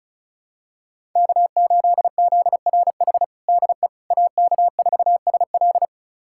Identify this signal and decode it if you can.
Morse code, heard as a keyed tone, about 36 wpm, 690 Hz. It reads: K8ZRH DE AK4SL
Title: USA Callsign DE Callsign - 5 Characters - 35wpm